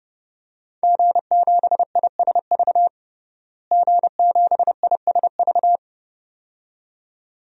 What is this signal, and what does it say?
Morse code, heard as a keyed tone, about 30 wpm, 700 Hz. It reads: G7ISV G7ISV
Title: DXCC-100 Call Signs - 5 Characters - Repeated 2x - 30wpm